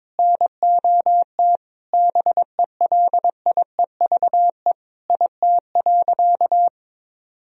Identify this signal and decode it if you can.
Morse code, heard as a keyed tone, about 22 wpm, 700 Hz. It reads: NOT BELIEVE IT.
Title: Sentences from Top 1000 Words - Review Part 3 - 22wpm